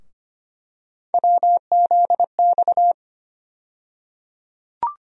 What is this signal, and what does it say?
Morse code, heard as a keyed tone, about 25 wpm, 700 Hz. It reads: WZX
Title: USA Call Sign Postfixes - 3 Letters 25wpm